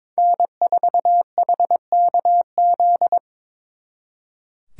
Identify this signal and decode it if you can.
Morse code, heard as a keyed tone, about 26 wpm, 700 Hz. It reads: N4HKZ
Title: Speed Racing - USA Call Signs 22wpm